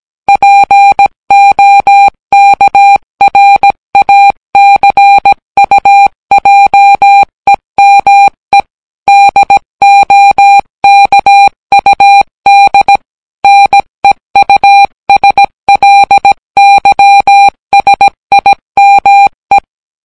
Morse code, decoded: POKRACUJEME DOKUD NEUSLYSIME